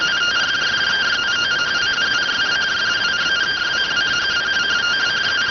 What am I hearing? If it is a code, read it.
Two-tone teleprinter signal, about 170 Hz shift.